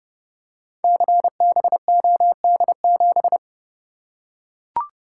Morse code, heard as a keyed tone, about 28 wpm, 700 Hz. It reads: CBOD7
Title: Code Groups - 5 Letter:Numbers - 30wpm